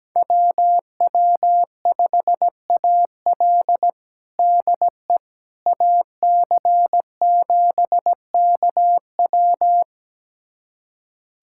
Morse code, heard as a keyed tone, about 17 wpm, 700 Hz. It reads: WW5AL DE AC7KW
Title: USA Callsign DE Callsign - 5 Characters - 17wpm